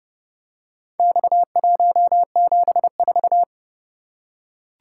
Morse code, heard as a keyed tone, about 29 wpm, 700 Hz. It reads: X174